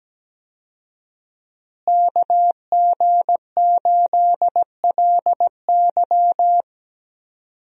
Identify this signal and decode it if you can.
Morse code, heard as a keyed tone, about 17 wpm, 700 Hz. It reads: KG8LY
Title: USA Call Signs - 5 Characters - Repeated 3x - 17wpm